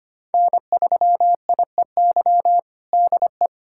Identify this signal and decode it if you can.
Morse code, heard as a keyed tone, about 25 wpm, 700 Hz. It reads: N3IEY DE
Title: USA Callsign DE Callsign - 5 Characters - 25wpm